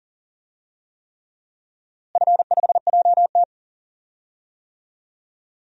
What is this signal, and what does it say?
Morse code, heard as a keyed tone, about 40 wpm, 700 Hz. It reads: F5JT
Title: DXCC-100 Call Signs - 4 Characters - Repeated 3x - 40wpm